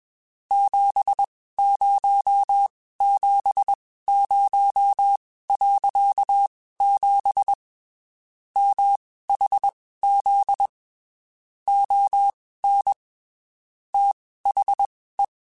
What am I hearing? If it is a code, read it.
Morse code, heard as a keyed tone, about 21 wpm, spaced as for 15 wpm, 800 Hz. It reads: 7070.7 MHZ ON THE
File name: (spaced) / QSOs Spaced-mp3s / WA7QOX 2 @20 sp